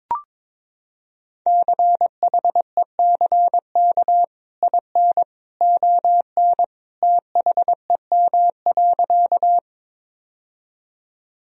Morse code, heard as a keyed tone, about 21 wpm, 700 Hz. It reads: CHECK IN ON THEM.
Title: Sentences from Top 1000 Words - Review Part 3 - 22wpm